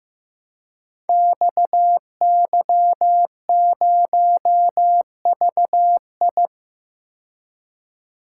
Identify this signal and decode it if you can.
Morse code, heard as a keyed tone, about 15 wpm, 700 Hz. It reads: XY0VI
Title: Code Groups - 5 Letter:Numbers - 15wpm